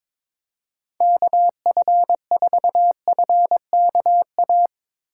Morse code, heard as a keyed tone, about 21 wpm, 700 Hz. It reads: KF4FKA
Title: USA Call Signs - 2x Character Spacing - 22wpm